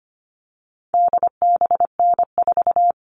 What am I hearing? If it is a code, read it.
Morse code, heard as a keyed tone, about 25 wpm, 700 Hz. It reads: DBN4
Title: All Cut Numbers - 4-Digit 25wpm